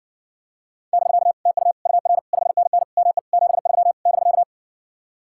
Morse code, heard as a keyed tone, about 60 wpm, 700 Hz. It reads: <BK> TU UR 5NN ME 73 <BK>